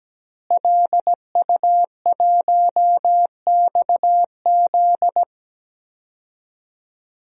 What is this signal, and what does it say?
Morse code, heard as a keyed tone, about 17 wpm, 700 Hz. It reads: LU1XZ
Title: DXCC-100 Call Signs - 5 Characters - Repeated 2x - 17wpm